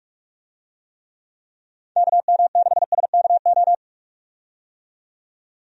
Morse code, heard as a keyed tone, about 46 wpm, 700 Hz. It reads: KM6SKO
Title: USA Call Signs - 6 Characters - 45wpm